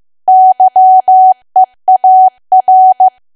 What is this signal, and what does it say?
Morse code, heard as a keyed tone, about 15 wpm, 750 Hz. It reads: YEAR
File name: cwa / W1AW-Level2 / 15WPM